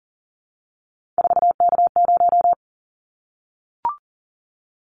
Morse code, heard as a keyed tone, about 40 wpm, 700 Hz. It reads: 4K0
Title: Callsign Prefixes for Top 100 DXCC - 40wpm